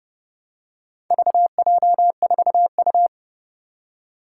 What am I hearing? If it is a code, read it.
Morse code, heard as a keyed tone, about 30 wpm, 700 Hz. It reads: VJ4U
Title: Four Challenging Letter-Number Combinations 30wpm